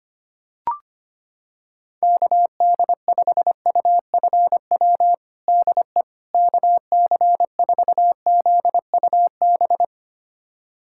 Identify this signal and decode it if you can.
Morse code, heard as a keyed tone, about 25 wpm, 700 Hz. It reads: KD5UFW DE KC4ZUB
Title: USA Callsign DE Callsign - 6 Characters - 25wpm